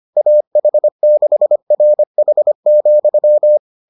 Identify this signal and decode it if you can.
Morse code, heard as a keyed tone, about 25 wpm, 600 Hz. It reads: AH6RH,